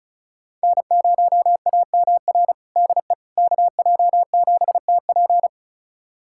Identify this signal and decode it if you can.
Morse code, heard as a keyed tone, about 34 wpm, 690 Hz. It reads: N0AMR DE KJ7TP
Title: USA Callsign DE Callsign - 5 Characters - 35wpm